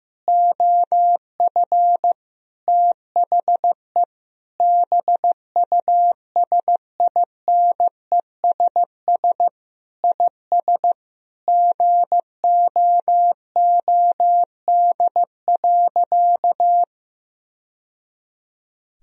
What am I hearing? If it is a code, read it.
Morse code, heard as a keyed tone, about 15 wpm, 700 Hz. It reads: OF THE BUSINESS IS GOOD.
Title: Sentences from Top 1000 Words - Review Part 4 - 15wpm